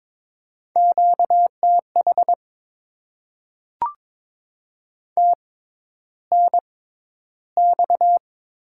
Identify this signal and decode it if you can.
Morse code, heard as a keyed tone, about 22 wpm, spaced as for 11 wpm, 700 Hz. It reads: QTH TNX
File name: Top 20 QSO Elements - 6x Character Spacing - 22wpm